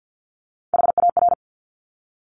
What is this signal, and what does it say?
Morse code, heard as a keyed tone, about 50 wpm, 690 Hz. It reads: HAR
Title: Top Three Letter Combinations in Words 50wpm